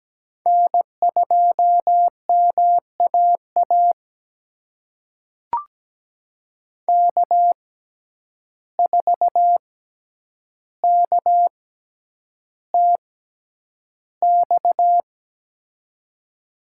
Morse code, heard as a keyed tone, about 17 wpm, spaced as for 9 wpm, 700 Hz. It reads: N2MAA K4KTX
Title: USA Call Signs - 6x Character Spacing - 17wpm